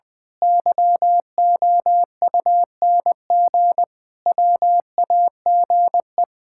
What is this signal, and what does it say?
Morse code, heard as a keyed tone, about 20 wpm, 700 Hz. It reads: YOUNG WAGE
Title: Pangram Challenge v2 - 20wpm